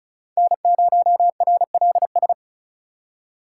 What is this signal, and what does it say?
Morse code, heard as a keyed tone, about 43 wpm, 700 Hz. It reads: N0RLS
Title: Speed Racing - USA Call Signs 35wpm